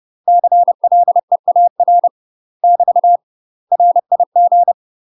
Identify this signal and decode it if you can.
Morse code, heard as a keyed tone, about 30 wpm, 700 Hz. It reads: CLEAR <BT> RIG
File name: UK MASTER QSOs TRAINING / 30WPM / UK MASTER QSOs 0009